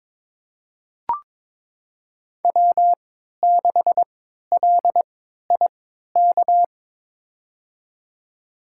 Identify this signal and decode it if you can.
Morse code, heard as a keyed tone, about 21 wpm, spaced as for 12 wpm, 700 Hz. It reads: W6LIK
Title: USA Call Signs - 5 Characters - 3x Character Spacing - 22wpm